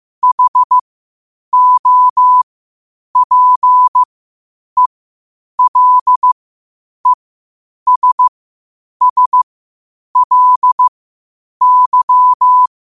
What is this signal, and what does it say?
Morse code, heard as a keyed tone, about 15 wpm, spaced as for 8 wpm, 1000 Hz. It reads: HOPELESSLY